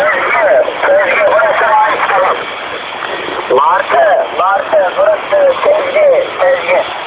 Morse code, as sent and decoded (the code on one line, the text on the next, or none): none